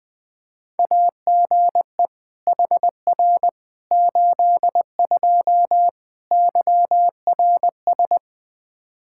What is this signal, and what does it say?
Morse code, heard as a keyed tone, about 20 wpm, 700 Hz. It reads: AGE HR 82 YRS
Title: K1USN SST QSO Practice 20wpm